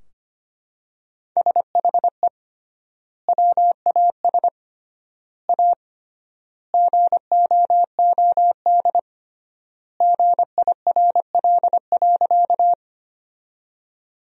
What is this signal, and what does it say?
Morse code, heard as a keyed tone, about 25 wpm, 700 Hz. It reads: SHE WAS A GOOD GIRL.
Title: 3x Word-Spacing - Sentences from Top 400 Words - 25wpm